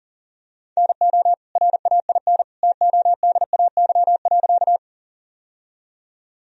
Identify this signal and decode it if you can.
Morse code, heard as a keyed tone, about 40 wpm, 700 Hz. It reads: NO RAIN TODAY.
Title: Sentences from Top 1000 Words - Review Part 1 - 40wpm